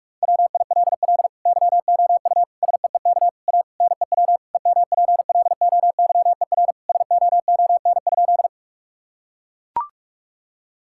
Morse code, heard as a keyed tone, about 42 wpm, 700 Hz. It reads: WILL YOU SEEK A NEW EMPLOYER SOON?
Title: Sentences from Top 1600 Words 45wpm